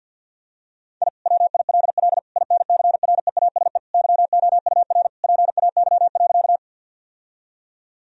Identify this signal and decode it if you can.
Morse code, heard as a keyed tone, about 52 wpm, 700 Hz. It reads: I WILL INCREASE YOUR PAY.
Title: Sentences from Top 1000 Words - Review Part 4 - 50wpm